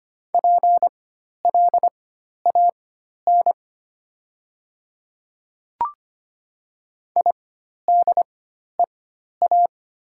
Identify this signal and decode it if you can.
Morse code, heard as a keyed tone, about 25 wpm, spaced as for 12 wpm, 700 Hz. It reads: PLAN IDEA
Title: 4x Character Spacing - Top 500 Words - 25wpm